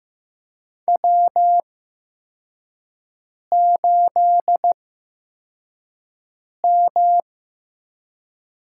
Morse code, heard as a keyed tone, about 15 wpm, spaced as for 4 wpm, 700 Hz. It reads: W8M
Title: USA Call Signs - 4 Characters - 8x Character Spacing - 15wpm